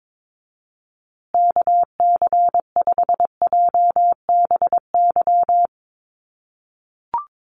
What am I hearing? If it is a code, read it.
Morse code, heard as a keyed tone, about 22 wpm, 700 Hz. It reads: KC5JBY
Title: USA Call Signs Repeated 3x 22wpm